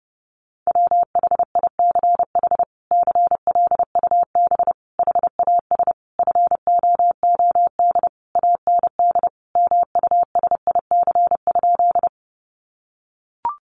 Morse code, heard as a keyed tone, about 30 wpm, 700 Hz. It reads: WHICH CLUB HAS FOOD AND MUSIC?
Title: Sentences from Top 1200 Words 30wpm